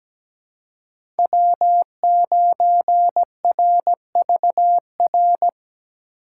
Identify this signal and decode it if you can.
Morse code, heard as a keyed tone, about 17 wpm, 700 Hz. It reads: W9RVR
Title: USA Call Signs - 5 Characters - 4x Character Spacing - 17wpm